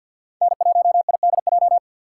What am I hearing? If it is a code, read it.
Morse code, heard as a keyed tone, about 52 wpm, 700 Hz. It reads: N1IDJ
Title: USA Call Signs - 6x Character Spacing - 50wpm